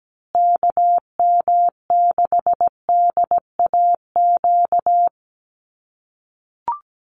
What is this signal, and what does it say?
Morse code, heard as a keyed tone, about 17 wpm, 700 Hz. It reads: KM6DAQ
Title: USA Call Signs Repeated 3x 17wpm